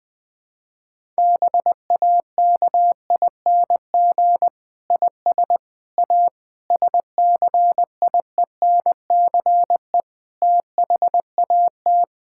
Morse code, heard as a keyed tone, about 20 wpm, 700 Hz. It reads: BAKING IS A SCIENCE THAT
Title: Difficult Sentences with Hints - 20wpm